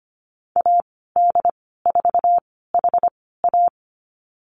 Morse code, heard as a keyed tone, about 25 wpm, spaced as for 16 wpm, 700 Hz. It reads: AD4HA
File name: Lesson 15g - Number 5 - Call Signs 25wpm